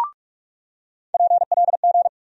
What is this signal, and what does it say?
Morse code, heard as a keyed tone, about 47 wpm, 700 Hz. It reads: PLG
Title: USA Call Sign Postfixes - 3 Letters 45wpm